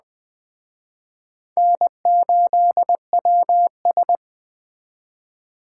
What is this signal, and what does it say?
Morse code, heard as a keyed tone, about 20 wpm, 700 Hz. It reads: N8WS